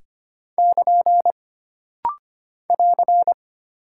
Morse code, heard as a keyed tone, about 25 wpm, 700 Hz. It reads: <KN> <AR>
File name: Single Punctuation-Prosign - Rapid-Fire - 25wpm